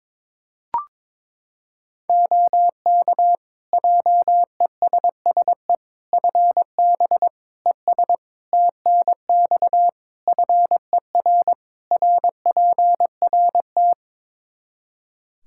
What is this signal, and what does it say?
Morse code, heard as a keyed tone, about 22 wpm, 700 Hz. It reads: OK JESSE FB ES TNX FER RPRT
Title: Practice QSO - Exchange 2 - Info - 22wpm